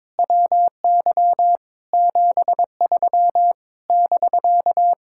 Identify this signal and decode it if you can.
Morse code, heard as a keyed tone, about 22 wpm, 700 Hz. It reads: WY 73 <BK>
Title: Speed-Racing Common QSO Phrases Encore - 22wpm